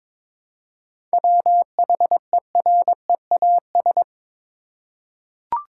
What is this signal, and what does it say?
Morse code, heard as a keyed tone, about 22 wpm, 700 Hz. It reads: WHEREAS
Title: Seven Letter Words - 22wpm